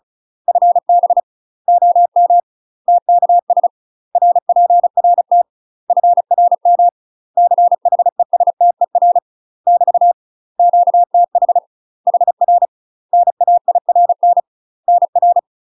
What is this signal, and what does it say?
Morse code, heard as a keyed tone, about 36 wpm, 700 Hz. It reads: FB OM TKS RPRT FRM CHESTER <BT> QTH HR NAIRN NR
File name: UK MASTER QSOs TRAINING / 35WPM / UK MASTER QSOs 0035